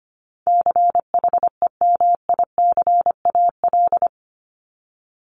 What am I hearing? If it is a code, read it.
Morse code, heard as a keyed tone, about 25 wpm, 700 Hz. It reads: CHEMICAL
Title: Top 1001-1250 Words 25wpm